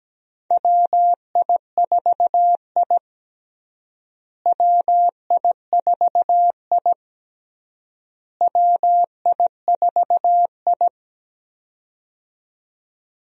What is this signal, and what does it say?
Morse code, heard as a keyed tone, about 17 wpm, 700 Hz. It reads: WI4I WI4I WI4I
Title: USA Call Signs - 4 Characters - Repeated 3x - 17wpm